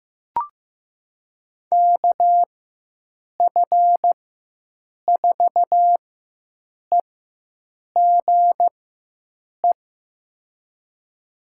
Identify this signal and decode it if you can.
Morse code, heard as a keyed tone, about 15 wpm, spaced as for 7 wpm, 700 Hz. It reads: KF4EGE
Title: USA Call Signs - 6 Characters - 4x Character Spacing - 15wpm